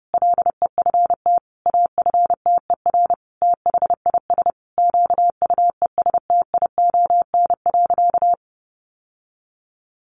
Morse code, heard as a keyed tone, about 30 wpm, 700 Hz. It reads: LEFT AFTER THIS QUESTION.
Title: Sentences from Top 1500 Words 30wpm